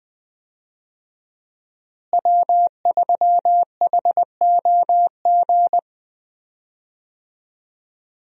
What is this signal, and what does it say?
Morse code, heard as a keyed tone, about 20 wpm, 700 Hz. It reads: W3HOG
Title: USA Call Signs - 5 Characters - Repeated 3x - 20wpm